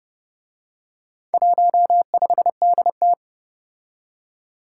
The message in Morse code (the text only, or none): 15DT